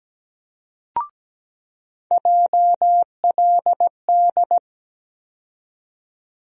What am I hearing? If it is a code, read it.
Morse code, heard as a keyed tone, about 17 wpm, 700 Hz. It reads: JLD
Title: USA Call Sign Postfixes - 3 Letters 17wpm